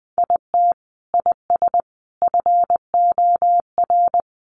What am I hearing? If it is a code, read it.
Morse code, heard as a keyed tone, about 20 wpm, 700 Hz. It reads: IT IS FOR
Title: Sentences from Top 700 Words 20wpm